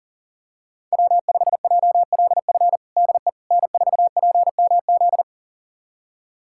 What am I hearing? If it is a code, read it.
Morse code, heard as a keyed tone, about 40 wpm, 700 Hz. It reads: W5JLF DE N4PMZ
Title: USA Callsign DE Callsign - 5 Characters - 40wpm